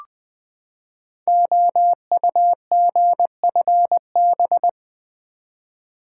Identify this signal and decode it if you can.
Morse code, heard as a keyed tone, about 20 wpm, 700 Hz. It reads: OUGFB
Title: Code Groups - 5 Letters - 20wpm